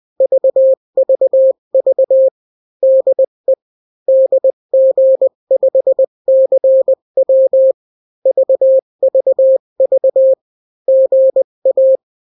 Morse code, heard as a keyed tone, about 20 wpm, 530 Hz. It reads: VVV DE DG5CW VVV GA
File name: daily / 18wpm